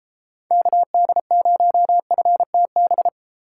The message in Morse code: KD0FTB

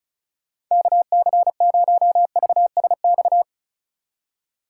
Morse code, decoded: KC0VSX

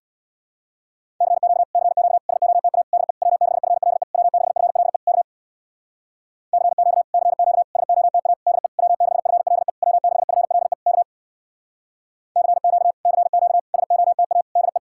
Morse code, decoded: CQ CQ SOTA DE W6FLE W6FLE K CQ CQ SOTA DE W6FLE W6FLE K CQ CQ SOTA DE

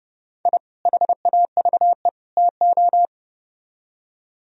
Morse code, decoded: I HAVE TO

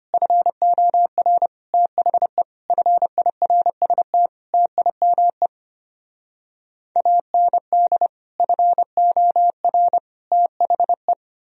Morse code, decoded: FOR THE FIRST TIME AND FOR THE

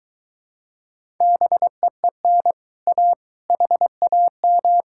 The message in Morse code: BEEN A HAM

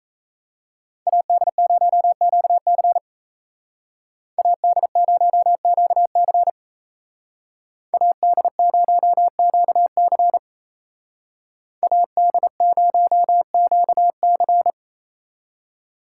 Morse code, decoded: AD0QC AD0QC AD0QC AD0QC

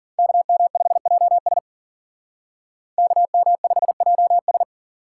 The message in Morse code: KM5JS KM5JS

